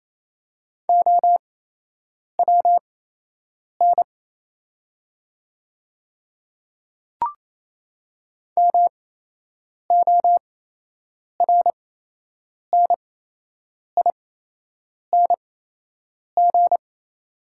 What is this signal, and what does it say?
Morse code, heard as a keyed tone, about 28 wpm, spaced as for 8 wpm, 700 Hz. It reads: OWN MORNING
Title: 8x Character Spacing - Top 500 Words - 28wpm